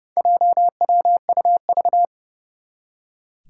Morse code, heard as a keyed tone, about 29 wpm, 700 Hz. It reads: JWUV
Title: Four Challenging Letter-Number Combinations 30wpm